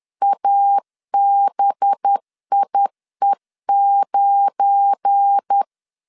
Morse code, heard as a keyed tone, about 10 wpm, 800 Hz. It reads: ABIE9
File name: LEZIONI / 7 GF9Z / 7 GF9 10wpm 2747